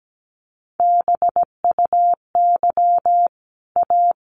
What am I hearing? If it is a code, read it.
Morse code, heard as a keyed tone, about 17 wpm, 700 Hz. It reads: BUY A
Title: Sentences from Top 1400 Words 17wpm